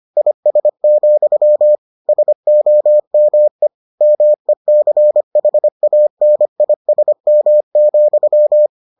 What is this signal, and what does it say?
Morse code, heard as a keyed tone, about 25 wpm, 600 Hz. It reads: IS, SOME MECHANISM,